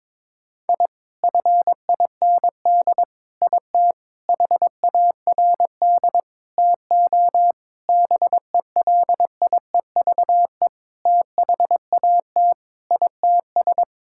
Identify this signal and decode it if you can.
Morse code, heard as a keyed tone, about 22 wpm, 700 Hz. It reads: I FIND IT HARD TO BELIEVE THAT ITS
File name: Sentences with 7-Word Phrases - 22wpm